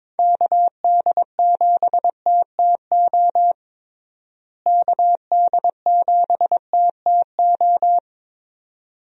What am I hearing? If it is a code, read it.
Morse code, heard as a keyed tone, about 21 wpm, 700 Hz. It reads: KD7TTO KD7TTO
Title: USA Call Signs - 6 Characters - Repeated 3x - 22wpm